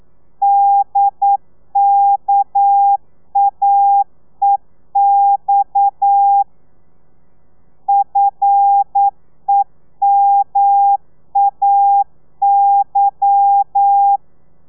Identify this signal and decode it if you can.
Morse code, heard as a keyed tone, about 9 wpm, 800 Hz. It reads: DKAEX FEMAY
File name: telegrafia / i2tfj / Lez11 40